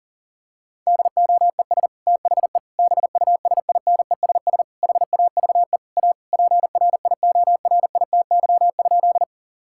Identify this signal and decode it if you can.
Morse code, heard as a keyed tone, about 40 wpm, 700 Hz. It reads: DOES THE BUSINESS HAVE A PRIORITY?
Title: Sentences from Top 1600 Words 40wpm